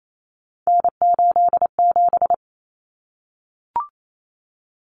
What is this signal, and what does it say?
Morse code, heard as a keyed tone, about 28 wpm, 700 Hz. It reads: N87
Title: Common Cut Numbers - 3-Digit 28wpm